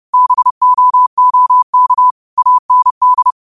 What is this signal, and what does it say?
Morse code, heard as a keyed tone, about 30 wpm, 1000 Hz. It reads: BOOK AND
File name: qotdfiles / 30wpm